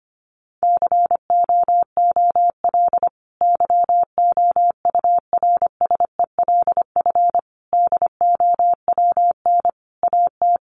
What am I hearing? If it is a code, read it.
Morse code, heard as a keyed tone, about 25 wpm, 700 Hz. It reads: COOL YOURSELF DOWN AT
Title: Sentences from Top 1200 Words 25wpm